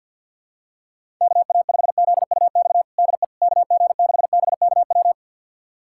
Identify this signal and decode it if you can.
Morse code, heard as a keyed tone, about 45 wpm, 700 Hz. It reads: KA5ZUX DE KG6BKW
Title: USA Callsign DE Callsign 50wpm